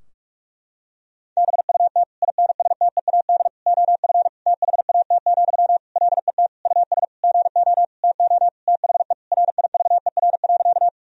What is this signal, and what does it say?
Morse code, heard as a keyed tone, about 46 wpm, 700 Hz. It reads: BUT INSTEAD OF THAT, LET US GO TO THE RIVER.
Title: Sentences from Top 1000 Words - Review Part 4 - 45wpm